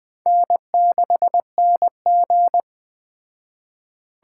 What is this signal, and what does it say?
Morse code, heard as a keyed tone, about 20 wpm, 700 Hz. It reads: N6NG